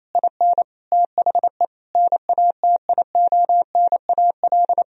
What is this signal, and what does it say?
Morse code, heard as a keyed tone, about 28 wpm, 700 Hz. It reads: IN THE NATIONAL
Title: Sentences from Top 1000 Words - Review Part 4 - 28wpm